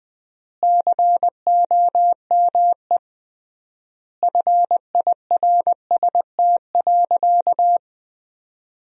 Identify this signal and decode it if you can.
Morse code, heard as a keyed tone, about 20 wpm, 700 Hz. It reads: COME FIRST.
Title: 3x Word-Spacing - Sentences from Top 300 Words - 20wpm